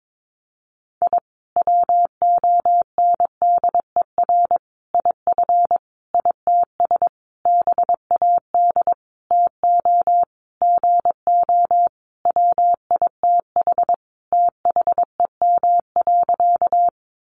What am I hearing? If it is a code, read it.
Morse code, heard as a keyed tone, about 22 wpm, 700 Hz. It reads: I WONDER IF ITS BAD TO GO WITH THEM.
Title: Sentences from Top 700 Words 22wpm